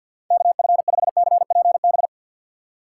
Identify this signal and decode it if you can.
Morse code, heard as a keyed tone, about 50 wpm, 700 Hz. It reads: KF5CPB
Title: USA Call Signs - 6 Characters - 6x Character Spacing - 50wpm